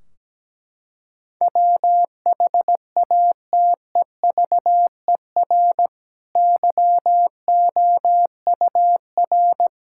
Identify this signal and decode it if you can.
Morse code, heard as a keyed tone, about 17 wpm, 700 Hz. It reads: WHATEVER YOUR
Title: Sentences from Top 1600 Words 17wpm